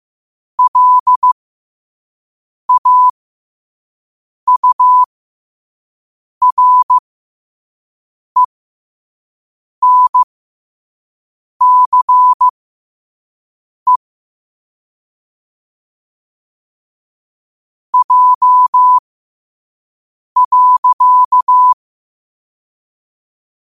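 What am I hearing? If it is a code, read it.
Morse code, heard as a keyed tone, about 15 wpm, spaced as for 5 wpm, 1000 Hz. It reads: LAURENCE J.
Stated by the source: Morse code sent at 5 wpm for learning.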